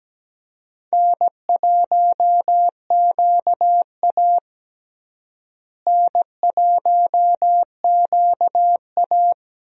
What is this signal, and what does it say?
Morse code, heard as a keyed tone, about 17 wpm, 700 Hz. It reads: N1QA N1QA
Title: USA Call Signs - 4 Characters - Repeated 2x - 17wpm